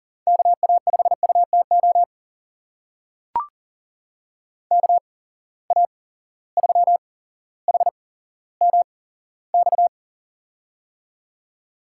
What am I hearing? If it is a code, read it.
Morse code, heard as a keyed tone, about 40 wpm, spaced as for 17 wpm, 700 Hz. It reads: KA5UTO KA3HMX
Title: USA Call Signs - 8x Character Spacing - 40wpm